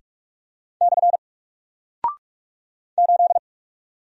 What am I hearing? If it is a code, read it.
Morse code, heard as a keyed tone, about 46 wpm, 700 Hz. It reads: / 8
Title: Single Character (Letter:Number:Punctuation:Prosign) Alt 45wpm